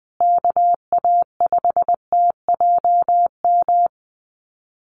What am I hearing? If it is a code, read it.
Morse code, heard as a keyed tone, about 20 wpm, 700 Hz. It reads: KA5TJM
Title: USA Call Signs Repeated 3x 20wpm